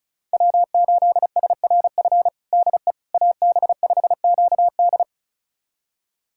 Morse code, heard as a keyed tone, about 36 wpm, 690 Hz. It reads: W8SRF DE AB5QD
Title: USA Callsign DE Callsign - 5 Characters - 35wpm